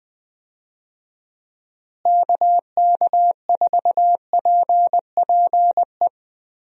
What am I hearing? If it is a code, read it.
Morse code, heard as a keyed tone, about 20 wpm, 700 Hz. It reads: KK4PPE
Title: USA Call Signs - 6 Characters - Repeated 3x - 20wpm